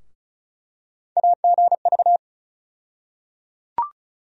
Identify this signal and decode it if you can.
Morse code, heard as a keyed tone, about 34 wpm, 700 Hz. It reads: AGV
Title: USA Call Sign Postfixes - 3 Letters 35wpm